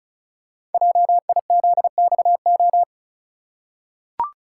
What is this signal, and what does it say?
Morse code, heard as a keyed tone, about 35 wpm, 700 Hz. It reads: JIZXO
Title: Code Groups - 5 Letters - 35wpm